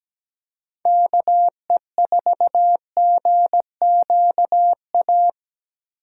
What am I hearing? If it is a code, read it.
Morse code, heard as a keyed tone, about 20 wpm, 700 Hz. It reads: KE4GQA
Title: Speed Racing - USA 2x3 Call Signs 17wpm